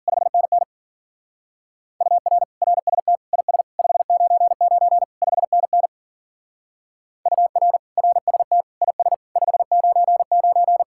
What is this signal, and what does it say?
Morse code, heard as a keyed tone, about 46 wpm, 700 Hz. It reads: 5NN UR RST IS 599 5NN UR RST IS 599
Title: Speed-Racing Common QSO Phrases - 40wpm